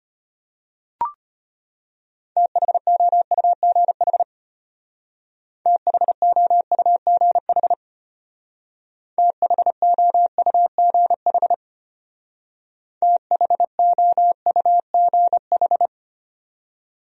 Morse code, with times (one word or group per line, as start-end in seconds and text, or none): 2.4-4.2 THOUGH
5.7-7.7 THOUGH
9.2-11.5 THOUGH
13.0-15.9 THOUGH